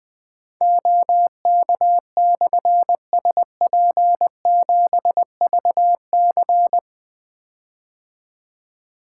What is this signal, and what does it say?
Morse code, heard as a keyed tone, about 20 wpm, 700 Hz. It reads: OK/SP7VC
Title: Difficult Call Signs 20wpm